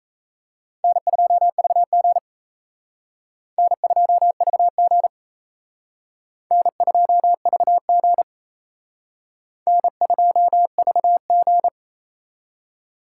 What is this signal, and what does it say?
Morse code, heard as a keyed tone, about 35 wpm, 700 Hz. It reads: N2VG N2VG N2VG N2VG